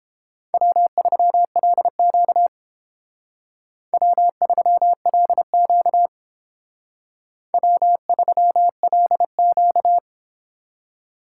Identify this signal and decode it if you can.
Morse code, heard as a keyed tone, about 28 wpm, 700 Hz. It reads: W3LQ W3LQ W3LQ